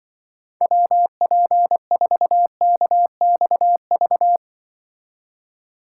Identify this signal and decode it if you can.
Morse code, heard as a keyed tone, about 23 wpm, 700 Hz. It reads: WP4KXV
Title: Speed Racing - USA 2x3 Call Signs 20wpm